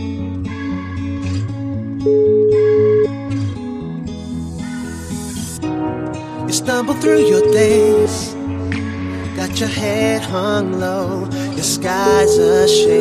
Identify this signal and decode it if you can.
Ringback tone (Europe, 425 Hz), with speech.